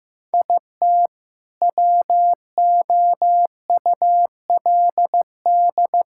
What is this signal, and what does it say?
Morse code, heard as a keyed tone, about 16 wpm, 700 Hz. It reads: IT WOULD
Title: Sentences from Top 1000 Words - Review Part 6 - 15wpm